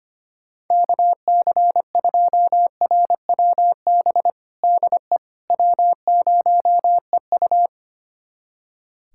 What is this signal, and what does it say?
Morse code, heard as a keyed tone, about 25 wpm, 700 Hz. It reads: KC2RWB DE W0EU
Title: USA Callsign DE Callsign 25wpm